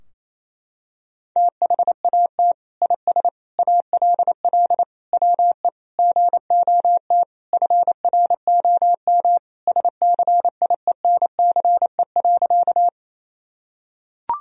Morse code, heard as a keyed tone, about 28 wpm, 700 Hz. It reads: THAT IS ALL WE GOT FROM SCIENCE.
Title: Sentences from Top 1000 Words - Review Part 2 - 28wpm